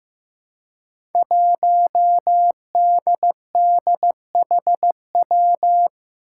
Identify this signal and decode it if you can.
Morse code, heard as a keyed tone, about 15 wpm, 700 Hz. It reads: 1DDHW
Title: Five Challenging Letter-Number Combinations 15wpm